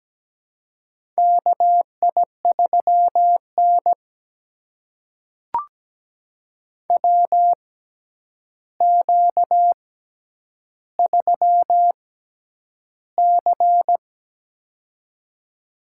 Morse code, heard as a keyed tone, about 17 wpm, spaced as for 9 wpm, 700 Hz. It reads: KI3N WQ3C
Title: USA Call Signs - 4 Characters - 6x Character Spacing - 17wpm